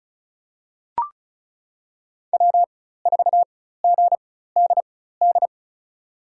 Morse code, heard as a keyed tone, about 34 wpm, spaced as for 16 wpm, 690 Hz. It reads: W4GDD
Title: USA Call Signs - 4x Character Spacing - 35wpm